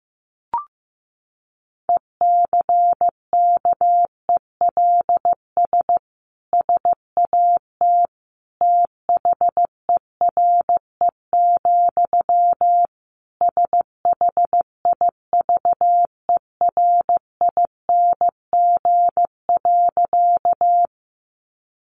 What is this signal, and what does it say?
Morse code, heard as a keyed tone, about 15 wpm, 700 Hz. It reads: ECKELS SAT THERE, SHIVERING.
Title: A Sound Of Thunder 15wpm